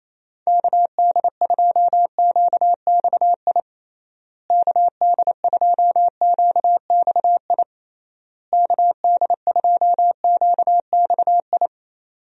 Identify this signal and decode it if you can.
Morse code, heard as a keyed tone, about 28 wpm, 700 Hz. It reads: KD2QXI KD2QXI KD2QXI
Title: USA Call Signs - 6 Characters - Repeated 3x - 28wpm